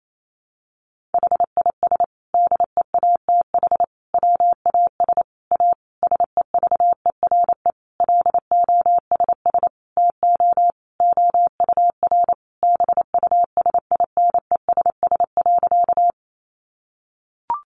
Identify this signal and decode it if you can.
Morse code, heard as a keyed tone, about 28 wpm, 700 Hz. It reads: HIS DEATH WAS A SEVERE LOSS TO OUR BUSINESS.
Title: Sentences from Top 1400 Words 28wpm